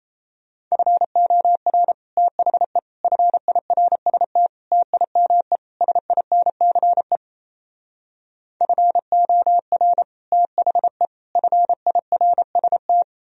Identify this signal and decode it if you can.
Morse code, heard as a keyed tone, about 38 wpm, 700 Hz. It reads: FOR THE FIRST TIME SINCE FOR THE FIRST